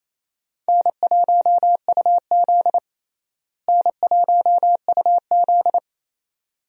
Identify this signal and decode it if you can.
Morse code, heard as a keyed tone, about 28 wpm, 700 Hz. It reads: N1UZ N1UZ
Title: USA Call Signs - 4 Characters - Repeated 2x - 28wpm